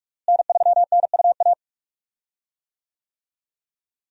Morse code, heard as a keyed tone, about 46 wpm, 700 Hz. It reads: N3NUA
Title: USA Call Signs - 5 Characters - Repeated 3x - 45wpm